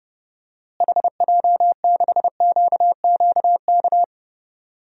Morse code, heard as a keyed tone, about 30 wpm, 700 Hz. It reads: HJ6QQK
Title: DXCC-100 Call Signs - 6 Characters - Repeated 3x - 30wpm